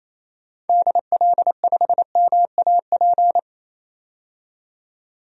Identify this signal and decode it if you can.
Morse code, heard as a keyed tone, about 28 wpm, 700 Hz. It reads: DL5MAP
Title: DXCC-100 Call Signs - 6 Characters - 28wpm